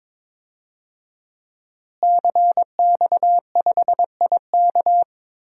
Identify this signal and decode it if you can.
Morse code, heard as a keyed tone, about 22 wpm, 700 Hz. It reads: CX5IK
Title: DXCC-100 Call Signs - 5 Characters - Repeated 2x - 22wpm